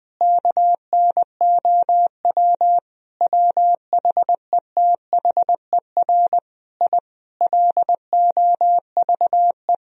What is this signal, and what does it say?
Morse code, heard as a keyed tone, about 20 wpm, 700 Hz. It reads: KNOW WHETHER I LOVE
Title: Sentences from Top 1000 Words - Review Part 3 - 20wpm